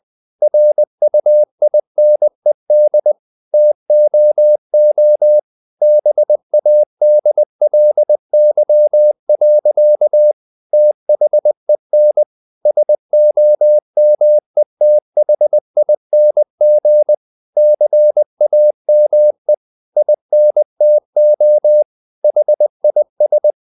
Morse code, heard as a keyed tone, about 20 wpm, 600 Hz. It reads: RUINED TOO BADLY. THEN SOMETHING CAME INTO HIS
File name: the old man and the sea part 2 20wpm